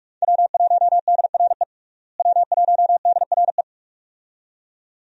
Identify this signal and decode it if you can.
Morse code, heard as a keyed tone, about 42 wpm, 700 Hz. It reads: W1DRE W1DRE
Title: USA Call Signs - 5 Characters - Repeated 2x - 45wpm